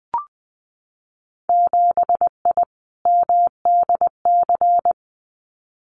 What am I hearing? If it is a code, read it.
Morse code, heard as a keyed tone, about 20 wpm, 700 Hz. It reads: 7I MDC
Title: Winter Field Day Exchanges Encore 20wpm